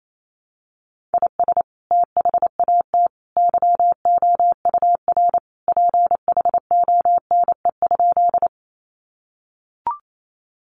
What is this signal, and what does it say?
Morse code, heard as a keyed tone, about 28 wpm, 700 Hz. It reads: IS THAT YOUR PHONE?
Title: Sentences from Top 1100 Words 28wpm